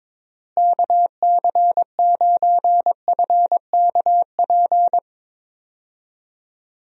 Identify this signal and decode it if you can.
Morse code, heard as a keyed tone, about 22 wpm, 700 Hz. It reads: KC9FKP
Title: USA Call Signs - 6 Characters - Repeated 2x - 22wpm